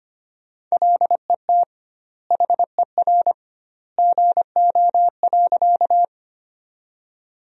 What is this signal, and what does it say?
Morse code, heard as a keyed tone, about 25 wpm, 700 Hz. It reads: LET HER GO.
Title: 2x Word-Spacing - Sentences from Top 300 Words - 25wpm